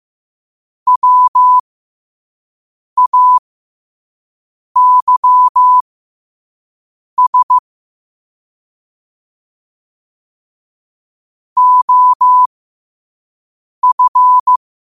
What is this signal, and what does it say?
Morse code, heard as a keyed tone, about 15 wpm, spaced as for 5 wpm, 1000 Hz. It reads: WAYS OF
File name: qotdfiles / 5wpm